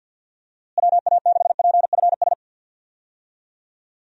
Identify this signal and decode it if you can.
Morse code, heard as a keyed tone, about 47 wpm, 700 Hz. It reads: WA6PFS
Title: USA Call Signs - 6 Characters - Repeated 2x - 50wpm